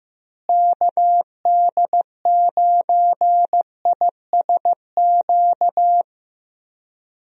Morse code, heard as a keyed tone, about 16 wpm, 700 Hz. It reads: KD9ISQ
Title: USA Call Signs - 6 Characters - Repeated 3x - 15wpm